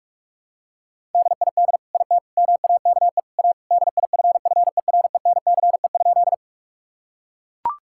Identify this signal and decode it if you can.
Morse code, heard as a keyed tone, about 41 wpm, 700 Hz. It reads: DID IT MAKE A DIFFERENCE?
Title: Sentences from Top 1000 Words - Review Part 5 - 45wpm